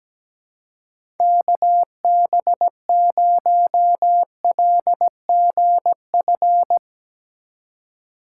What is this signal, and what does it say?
Morse code, heard as a keyed tone, about 17 wpm, 700 Hz. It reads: KB0LGF
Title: K1USN SST QSO Practice - 17wpm